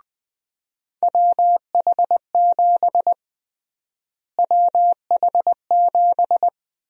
Morse code, heard as a keyed tone, about 21 wpm, 700 Hz. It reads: WH7 WH7
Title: USA Call Sign Prefixes - Repeated 2x - 20wpm